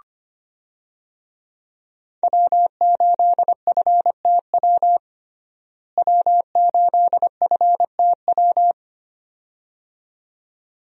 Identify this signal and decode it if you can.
Morse code, heard as a keyed tone, about 25 wpm, 700 Hz. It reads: W8FTW W8FTW
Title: USA Call Signs - 5 Characters - Repeated 2x - 25wpm